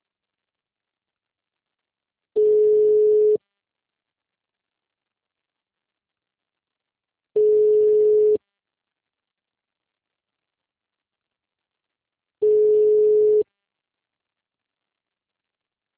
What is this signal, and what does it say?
Ringback tone (Europe, 425 Hz).